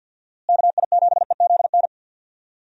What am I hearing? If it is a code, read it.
Morse code, heard as a keyed tone, about 53 wpm, 700 Hz. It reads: KI7EZN
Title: USA Call Signs - 6 Characters - Repeated 2x - 50wpm